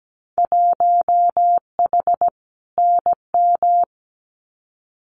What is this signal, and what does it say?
Morse code, heard as a keyed tone, about 17 wpm, 700 Hz. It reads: 1H NM
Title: Winter Field Day Exchanges 17wpm